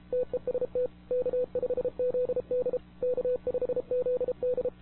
Morse code, heard as a keyed tone, about 32 wpm, 500 Hz. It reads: TEST K5ZD K5ZD
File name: wwcw08 / k5zd wwcw08 1700z